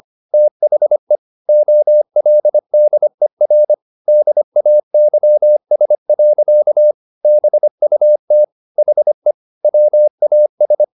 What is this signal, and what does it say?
Morse code, heard as a keyed tone, about 25 wpm, 600 Hz. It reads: THE OLDER DAYS. BUT HE WAS
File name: the old man and the sea part 1 25wpm